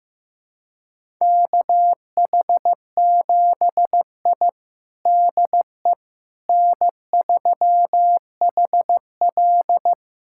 Morse code, heard as a keyed tone, about 15 wpm, 700 Hz. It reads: KH7I DE N3HL
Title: USA Callsign DE Callsign - 4 Characters - 15wpm